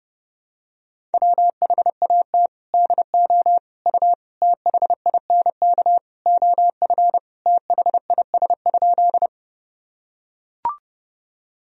Morse code, heard as a keyed tone, about 30 wpm, 700 Hz. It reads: WHAT DO U THINK OF THIS?
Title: Sentences from Top 200 Words and Abbreviations 30wpm